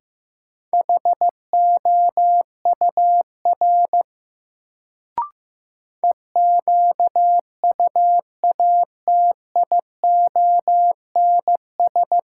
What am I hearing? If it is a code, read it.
Morse code, heard as a keyed tone, about 15 wpm, 700 Hz. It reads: HOUR EQUATIONS
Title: Sets of 5 Words - New Word Drill - 15wpm